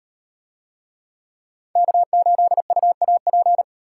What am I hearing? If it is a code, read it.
Morse code, heard as a keyed tone, about 36 wpm, 700 Hz. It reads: K8UAP